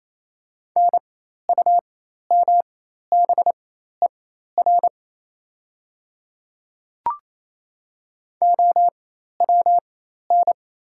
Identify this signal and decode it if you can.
Morse code, heard as a keyed tone, about 28 wpm, spaced as for 13 wpm, 690 Hz. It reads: NUMBER OWN
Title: 4x Character Spacing - Top 500 Words - 28wpm